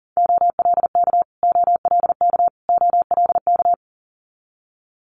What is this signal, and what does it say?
Morse code, heard as a keyed tone, about 43 wpm, 700 Hz. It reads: OLK OLK OLK
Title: Multi-letter Phonemes Repeated 3x 40wpm